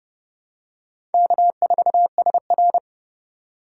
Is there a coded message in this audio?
Yes: Morse code from 1.1 to 2.8 s, heard as a keyed tone, about 28 wpm, 700 Hz. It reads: K4SR